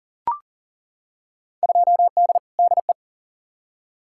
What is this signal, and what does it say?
Morse code, heard as a keyed tone, about 40 wpm, 700 Hz. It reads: 2D DE
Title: Field Day Exchanges Encore - 40wpm